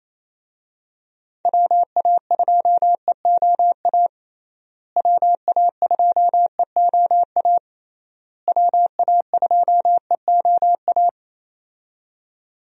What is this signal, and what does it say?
Morse code, heard as a keyed tone, about 28 wpm, 700 Hz. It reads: WA2EOA WA2EOA WA2EOA
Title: USA Call Signs - 6 Characters - Repeated 3x - 28wpm